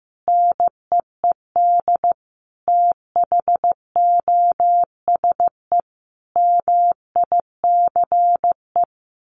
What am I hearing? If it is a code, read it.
Morse code, heard as a keyed tone, about 15 wpm, 700 Hz. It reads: NEED THOSE MICE
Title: A Sound Of Thunder 15wpm